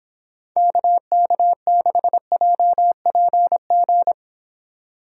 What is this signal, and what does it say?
Morse code, heard as a keyed tone, about 29 wpm, 690 Hz. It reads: KK6JPG